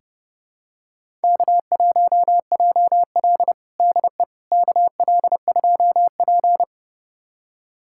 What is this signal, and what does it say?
Morse code, heard as a keyed tone, about 31 wpm, 700 Hz. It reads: K1JL DE KL2P
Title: USA Callsign DE Callsign - 4 Characters - 30wpm